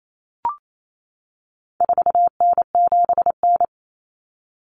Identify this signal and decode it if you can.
Morse code, heard as a keyed tone, about 28 wpm, 700 Hz. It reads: 4N7N
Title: Common Cut Numbers - 4-Digit 28wpm